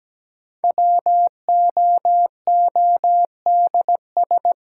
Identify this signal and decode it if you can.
Morse code, heard as a keyed tone, about 17 wpm, 700 Hz. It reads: WOODS
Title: Sentences from Top 1000 Words - Review Part 2 - New Word Drill - 17wpm